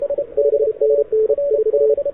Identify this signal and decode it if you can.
Morse code, heard as a keyed tone, about 33 wpm, 440 Hz, with radio noise. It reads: 5NN 4
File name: audio zawody / CQWW 2014 CW SN7Q